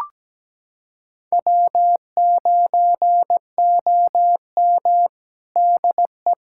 Morse code, heard as a keyed tone, about 17 wpm, 700 Hz. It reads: W9OM DE
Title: USA Callsign DE Callsign - 4 Characters - 17wpm